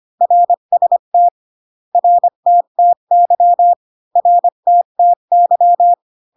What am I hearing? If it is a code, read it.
Morse code, heard as a keyed tone, about 25 wpm, 700 Hz. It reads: RST RTTY RTTY